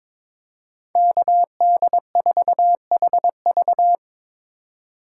Morse code, heard as a keyed tone, about 22 wpm, 700 Hz. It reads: KD4HV
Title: USA Call Signs - 5 Characters - 3x Character Spacing - 22wpm